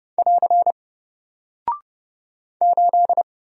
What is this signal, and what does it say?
Morse code, heard as a keyed tone, about 30 wpm, 700 Hz. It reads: <AR> 8
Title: Single Character (Letter:Number:Punctuation:Prosign) Alt 30wpm